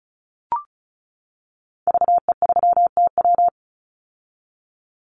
Morse code, heard as a keyed tone, about 36 wpm, 700 Hz. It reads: VE3TW
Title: Call Signs - 35wpm